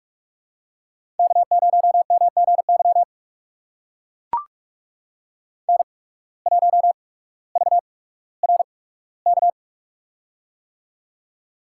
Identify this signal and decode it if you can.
Morse code, heard as a keyed tone, about 44 wpm, spaced as for 19 wpm, 700 Hz. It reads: K0MGY N1VRK
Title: USA Call Signs - 5 Characters - 8x Character Spacing - 45wpm